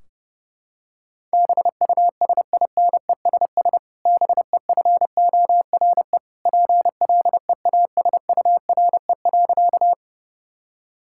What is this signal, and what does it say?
Morse code, heard as a keyed tone, about 32 wpm, 700 Hz. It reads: BUSINESS BEFORE PLEASURE.
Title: Sentences from Top 1600 Words 30wpm